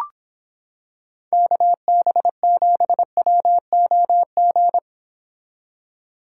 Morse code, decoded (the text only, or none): KB7WOG